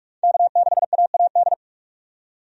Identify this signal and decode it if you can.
Morse code, heard as a keyed tone, about 42 wpm, 700 Hz. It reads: K6AAD
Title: USA Call Signs - 5 Characters - 45wpm